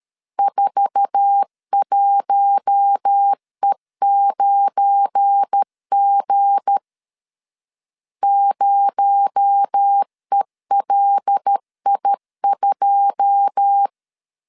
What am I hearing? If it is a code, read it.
Morse code, heard as a keyed tone, about 13 wpm, 800 Hz. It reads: 41E9G 0ELI2
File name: CW / DET / 1 9 1500C S C 12wpm 3811